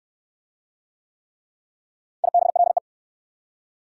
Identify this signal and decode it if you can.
Morse code, heard as a keyed tone, about 61 wpm, 690 Hz. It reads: IBLE